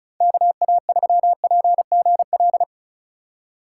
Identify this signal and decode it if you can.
Morse code, heard as a keyed tone, about 36 wpm, 700 Hz. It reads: KA3PGL
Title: USA Call Signs - 6 Characters - 4x Character Spacing - 35wpm